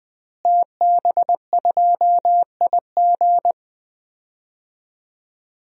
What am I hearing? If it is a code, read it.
Morse code, heard as a keyed tone, about 21 wpm, 700 Hz. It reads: TB2IG
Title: DXCC-100 Call Signs - 5 Characters - Repeated 3x - 20wpm